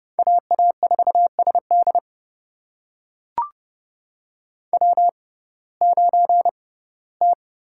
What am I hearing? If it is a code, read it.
Morse code, heard as a keyed tone, about 30 wpm, spaced as for 18 wpm, 700 Hz. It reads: AA4SD W9T
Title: USA Call Signs - 5 Characters - 6x Character Spacing - 30wpm